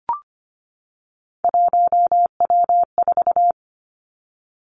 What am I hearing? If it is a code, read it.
Morse code, heard as a keyed tone, about 25 wpm, 700 Hz. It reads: 1W4
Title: Three Challenging Letter-Number Combinations 25wpm